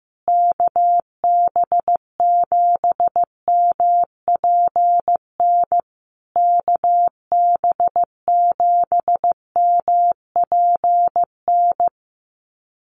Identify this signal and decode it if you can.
Morse code, heard as a keyed tone, about 15 wpm, 700 Hz. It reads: KB7MPN KB7MPN
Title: USA Call Signs Repeated 3x 15wpm